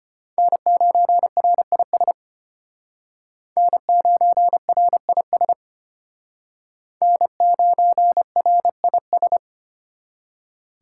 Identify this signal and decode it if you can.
Morse code, heard as a keyed tone, about 29 wpm, 700 Hz. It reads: N9RIS N9RIS N9RIS
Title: Speed Racing - USA Call Signs 25wpm